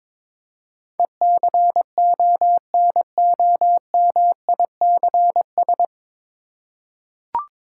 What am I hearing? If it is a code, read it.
Morse code, heard as a keyed tone, about 22 wpm, 700 Hz. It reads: ECONOMICS
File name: 9 Letter Words - First 3 Letters Given - 22wpm